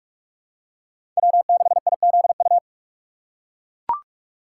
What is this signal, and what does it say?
Morse code, heard as a keyed tone, about 44 wpm, 690 Hz. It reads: W6IZU
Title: USA Call Signs - 5 Characters - 4x Character Spacing - 45wpm